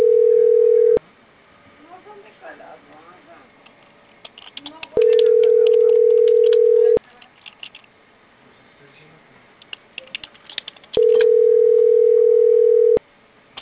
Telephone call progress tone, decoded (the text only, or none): ringback tone